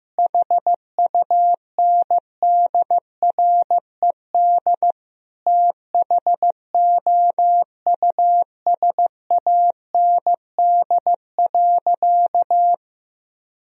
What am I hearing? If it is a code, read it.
Morse code, heard as a keyed tone, about 15 wpm, 700 Hz. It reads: HUNDRED THOUSAND.
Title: Sentences from Top 1000 Words - Review Part 3 - 15wpm